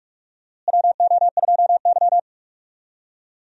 Morse code, heard as a keyed tone, about 46 wpm, 700 Hz. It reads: WO2Y
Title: USA Call Signs - 4 Characters - Repeated 3x - 45wpm